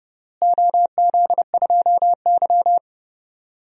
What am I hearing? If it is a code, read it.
Morse code, heard as a keyed tone, about 30 wpm, 700 Hz. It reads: OZ2Y
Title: DXCC-100 Call Signs - 4 Characters - Repeated 3x - 30wpm